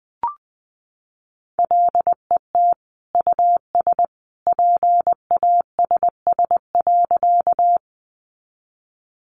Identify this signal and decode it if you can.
Morse code, heard as a keyed tone, about 20 wpm, 700 Hz. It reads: LET US PASS.
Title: Sentences from Top 400 Words 20wpm